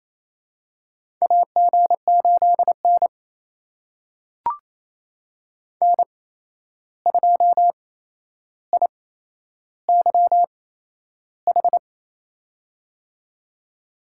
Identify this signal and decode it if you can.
Morse code, heard as a keyed tone, about 28 wpm, spaced as for 11 wpm, 700 Hz. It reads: AG8N N2IYH
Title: USA Call Signs - 8x Character Spacing - 28wpm